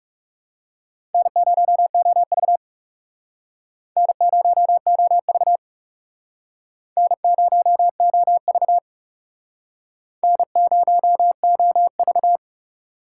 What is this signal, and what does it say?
Morse code, heard as a keyed tone, about 37 wpm, 700 Hz. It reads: N0OV N0OV N0OV N0OV
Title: Speed Racing - USA 1x2 Call Signs 30wpm